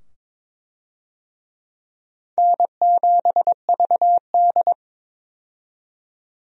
Morse code, heard as a keyed tone, about 22 wpm, 700 Hz. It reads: N7VD
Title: USA Call Signs - 4 Characters - 22wpm